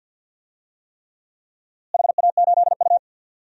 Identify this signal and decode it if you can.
Morse code, heard as a keyed tone, about 52 wpm, 700 Hz. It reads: HA8U
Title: DXCC-100 Call Signs - 4 Characters - Repeated 3x - 50wpm